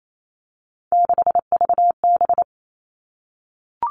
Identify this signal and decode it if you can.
Morse code, heard as a keyed tone, about 28 wpm, 700 Hz. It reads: 6VB
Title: All Cut Numbers - 3-Digit 28wpm